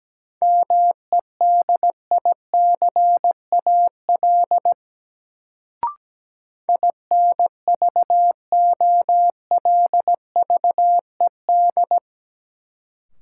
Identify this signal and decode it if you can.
Morse code, heard as a keyed tone, about 17 wpm, 700 Hz. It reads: MEDICAL INVOLVED
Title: Sentences from Top 800 Words - New Word Drill - 17wpm